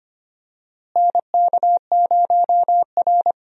Morse code, heard as a keyed tone, about 25 wpm, 700 Hz. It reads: NK0R
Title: USA Call Signs - 4 Characters - Repeated 2x - 25wpm